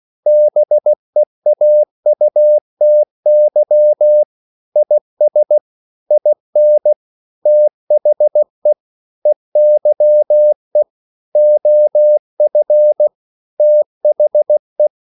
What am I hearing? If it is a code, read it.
Morse code, heard as a keyed tone, about 16 wpm, 600 Hz. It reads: BEAUTY IS IN THE EYE OF THE
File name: CW / Practice / EnglishSayings16